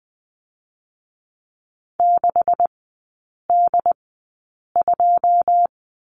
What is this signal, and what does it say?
Morse code, heard as a keyed tone, about 20 wpm, spaced as for 8 wpm, 690 Hz. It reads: 6D2
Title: 133 Lesson 22e - Question Mark - Three Characters 20wpm